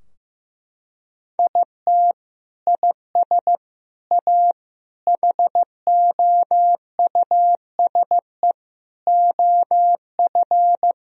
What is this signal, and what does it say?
Morse code, heard as a keyed tone, about 14 wpm, 700 Hz. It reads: IT IS A HOUSE OF
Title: Sentences from Top 1000 Words - Review Part 3 - 15wpm